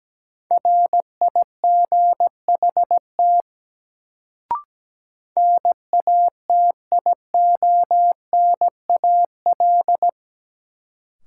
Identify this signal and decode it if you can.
Morse code, heard as a keyed tone, about 17 wpm, 700 Hz. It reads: RIGHT NATIONAL
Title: Sets of 2 Words - New Word Drill - 17wpm